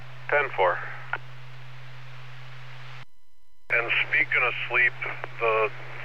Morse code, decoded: none

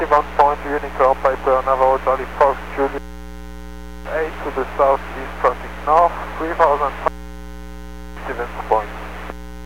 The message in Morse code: none

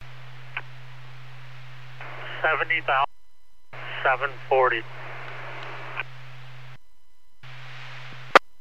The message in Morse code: none

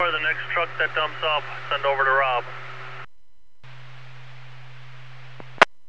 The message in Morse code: none

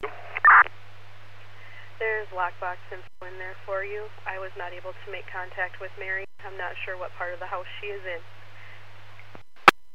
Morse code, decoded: none

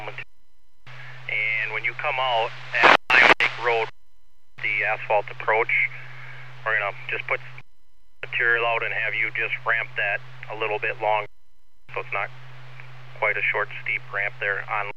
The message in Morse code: none

